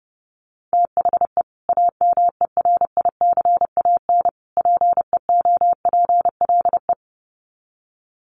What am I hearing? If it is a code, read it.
Morse code, heard as a keyed tone, about 30 wpm, 700 Hz. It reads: THE AMERICAN PEOPLE
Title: Sets of 3 Words 30wpm